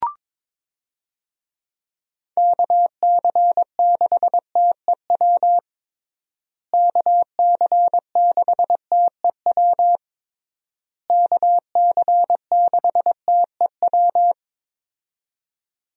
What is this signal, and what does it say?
Morse code, heard as a keyed tone, about 22 wpm, 700 Hz. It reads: KC6TEW KC6TEW KC6TEW
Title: USA Call Signs - 6 Characters - Repeated 3x - 22wpm